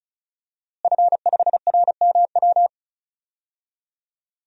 Morse code, heard as a keyed tone, about 36 wpm, 700 Hz. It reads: F5RMW
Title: DXCC-100 Call Signs - 5 Characters - Repeated 3x - 35wpm